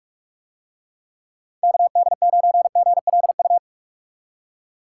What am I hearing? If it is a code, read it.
Morse code, heard as a keyed tone, about 46 wpm, 700 Hz. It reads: KD9GLU